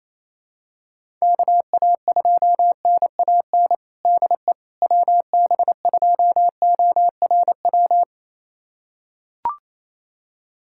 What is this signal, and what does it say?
Morse code, heard as a keyed tone, about 28 wpm, 700 Hz. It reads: KA2NAN DE WB2ORW
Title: USA Callsign DE Callsign 28wpm